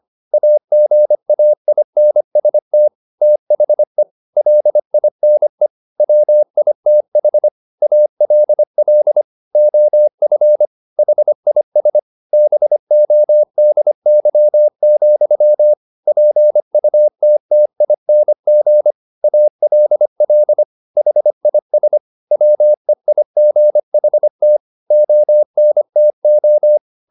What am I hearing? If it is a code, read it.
Morse code, heard as a keyed tone, about 25 wpm, 600 Hz. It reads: AGAINST THE LINE WITH ALL OF HIS BODY, PUTTING ALL HIS WEIGHT ONTO